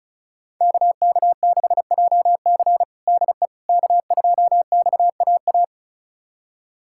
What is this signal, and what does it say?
Morse code, heard as a keyed tone, about 36 wpm, 700 Hz. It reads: KK6JC DE K2XAA
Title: USA Callsign DE Callsign - 5 Characters - 35wpm